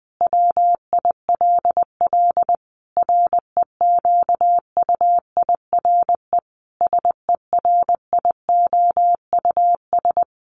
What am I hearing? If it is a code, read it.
Morse code, heard as a keyed tone, about 19 wpm, 700 Hz. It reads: WILL REQUIRE SERIOUS
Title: Sentences from Top 1500 Words 20wpm